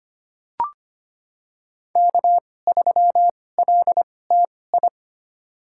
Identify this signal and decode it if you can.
Morse code, heard as a keyed tone, about 25 wpm, spaced as for 18 wpm, 700 Hz. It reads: K3LTI
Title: USA Call Signs - 5 Characters - 2x Character Spacing - 25wpm